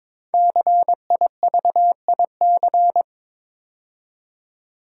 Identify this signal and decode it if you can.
Morse code, heard as a keyed tone, about 22 wpm, 690 Hz. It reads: CIVIC
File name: 5 Letter Words - First 3 Letters Given - 22wpm